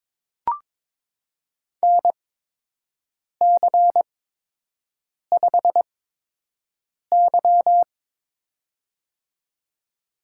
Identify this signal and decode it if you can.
Morse code, heard as a keyed tone, about 22 wpm, spaced as for 6 wpm, 700 Hz. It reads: NC5Y
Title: USA Call Signs - 8x Character Spacing - 22wpm